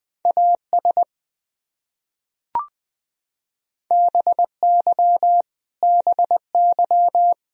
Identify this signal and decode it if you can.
Morse code, heard as a keyed tone, about 20 wpm, 700 Hz. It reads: AS BY BY
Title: Top 100 Words Repeated 20wpm